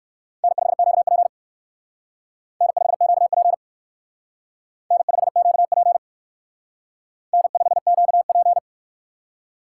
Morse code, heard as a keyed tone, about 56 wpm, 700 Hz. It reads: N5QP N5QP N5QP N5QP